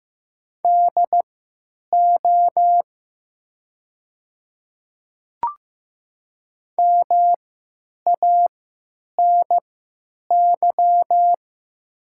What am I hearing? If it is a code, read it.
Morse code, heard as a keyed tone, about 15 wpm, spaced as for 9 wpm, 700 Hz. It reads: DO MANY
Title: 3x Character Spacing - Top 300 Words - 15wpm